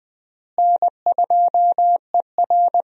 Morse code, heard as a keyed tone, about 20 wpm, 700 Hz. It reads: N2ER
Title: USA Call Signs - 4 Characters - Repeated 3x - 20wpm